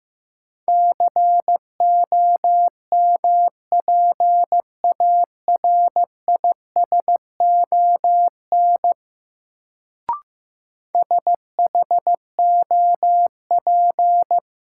Morse code, heard as a keyed tone, about 16 wpm, 700 Hz. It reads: COMPARISON SHOP
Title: Sentences from Top 1600 Words - New Word Drill Alt - 15wpm